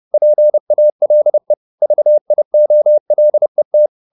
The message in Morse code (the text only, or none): PALE VIOLET